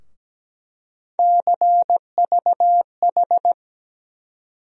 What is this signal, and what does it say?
Morse code, heard as a keyed tone, about 17 wpm, 700 Hz. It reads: CVH
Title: USA Call Sign Postfixes - 3 Letters 17wpm